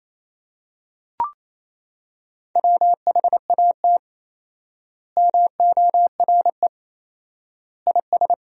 Morse code, heard as a keyed tone, about 28 wpm, 700 Hz. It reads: WHAT MORE IS
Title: 4x Word-Spacing - Sentences from Top 500 Words - 28wpm